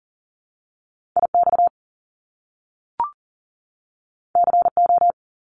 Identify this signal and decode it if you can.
Morse code, heard as a keyed tone, about 40 wpm, 700 Hz. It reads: IX CO
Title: Top Two Letter Combinations in Words 40wpm